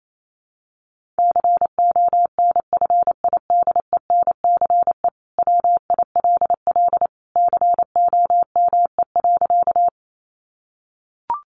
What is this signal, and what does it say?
Morse code, heard as a keyed tone, about 28 wpm, 700 Hz. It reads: CONFIDENCE WILL COME.
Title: Sentences from Top 1500 Words 28wpm